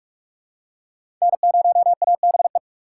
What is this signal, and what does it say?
Morse code, heard as a keyed tone, about 45 wpm, 700 Hz. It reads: N0ABE
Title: USA Call Signs - 5 Characters - 3x Character Spacing - 45wpm